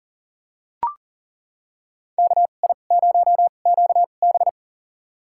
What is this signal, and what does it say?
Morse code, heard as a keyed tone, about 40 wpm, spaced as for 29 wpm, 700 Hz. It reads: KI0QB
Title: USA Call Signs - 5 Characters - 2x Character Spacing - 40wpm